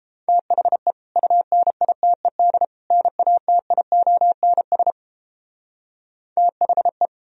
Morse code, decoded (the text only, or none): THE UNITED NATIONS THE